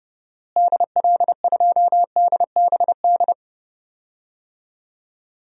Morse code, heard as a keyed tone, about 30 wpm, 700 Hz. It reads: DL2DBD
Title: DXCC-100 Call Signs - 6 Characters - Repeated 2x - 30wpm